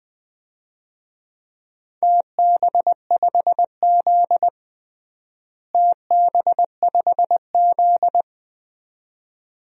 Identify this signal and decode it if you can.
Morse code, heard as a keyed tone, about 20 wpm, 700 Hz. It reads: TB5Z TB5Z
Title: DXCC-100 Call Signs - 4 Characters - Repeated 2x - 20wpm